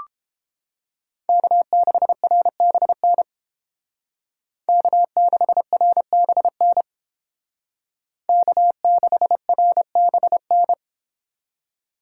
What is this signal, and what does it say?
Morse code, heard as a keyed tone, about 31 wpm, 700 Hz. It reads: K6RBN K6RBN K6RBN